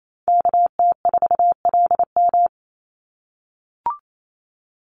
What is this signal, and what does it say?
Morse code, heard as a keyed tone, about 28 wpm, 700 Hz. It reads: KT4LM
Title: USA Call Signs 28wpm